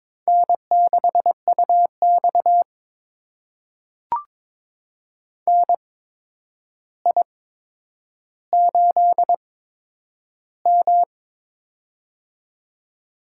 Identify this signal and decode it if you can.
Morse code, heard as a keyed tone, about 22 wpm, spaced as for 9 wpm, 700 Hz. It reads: N6UX NI8M
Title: USA Call Signs - 4 Characters - 8x Character Spacing - 22wpm